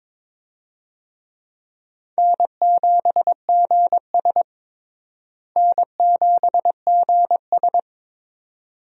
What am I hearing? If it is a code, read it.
Morse code, heard as a keyed tone, about 21 wpm, 700 Hz. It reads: N7GS N7GS
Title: USA Call Signs - 4 Characters - Repeated 2x - 22wpm